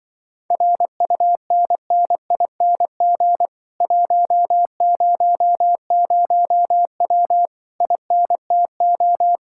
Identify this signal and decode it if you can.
Morse code, heard as a keyed tone, about 22 wpm, 700 Hz. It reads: RUNNING 100W INTO
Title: Speed-Racing Common QSO Phrases - 20wpm